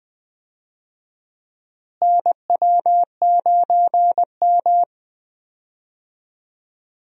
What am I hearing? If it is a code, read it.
Morse code, heard as a keyed tone, about 20 wpm, 700 Hz. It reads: NW9M
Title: USA Call Signs - 4 Characters - Repeated 2x - 20wpm